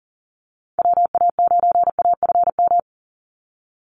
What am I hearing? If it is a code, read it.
Morse code, heard as a keyed tone, about 40 wpm, 700 Hz. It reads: WA9AFM
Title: USA Call Signs 40wpm